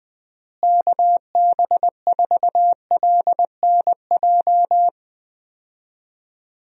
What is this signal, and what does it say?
Morse code, heard as a keyed tone, about 20 wpm, 700 Hz. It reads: KB4LNJ
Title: USA Call Signs - 6 Characters - Repeated 3x - 20wpm